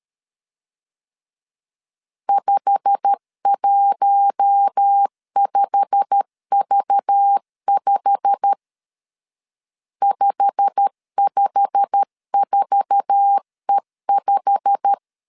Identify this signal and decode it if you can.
Morse code, heard as a keyed tone, about 13 wpm, 800 Hz. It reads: 515V5 554E5
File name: LEZIONI / 8 YJ45 / 8 YJ45 12wpm 2151